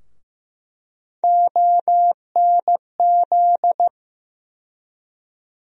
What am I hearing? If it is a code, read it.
Morse code, heard as a keyed tone, about 15 wpm, 700 Hz. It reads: ONZ
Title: USA Call Sign Postfixes - 3 Letters 15wpm